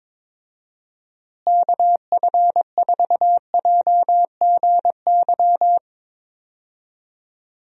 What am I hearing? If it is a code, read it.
Morse code, heard as a keyed tone, about 22 wpm, 700 Hz. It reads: KF4JGY
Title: USA Call Signs - 6 Characters - Repeated 2x - 22wpm